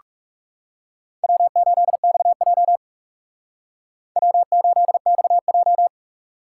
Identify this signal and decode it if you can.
Morse code, heard as a keyed tone, about 40 wpm, 700 Hz. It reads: W8XJ W8XJ